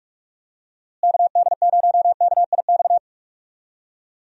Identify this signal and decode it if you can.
Morse code, heard as a keyed tone, about 46 wpm, 700 Hz. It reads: KD0KIX
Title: USA Call Signs - 6 Characters - Repeated 3x - 45wpm